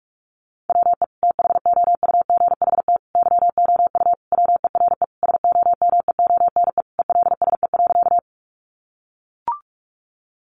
Morse code, heard as a keyed tone, about 44 wpm, 700 Hz. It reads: WE THOUGHT YOU WERE SOMEONE ELSE.
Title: Sentences from Top 800 Words 45wpm